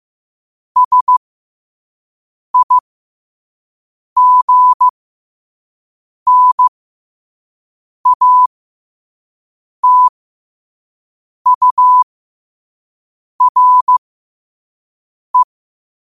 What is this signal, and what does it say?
Morse code, heard as a keyed tone, about 15 wpm, spaced as for 5 wpm, 1000 Hz. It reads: SIGNATURE
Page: New quotes every day in morse code at 5 Words per minute.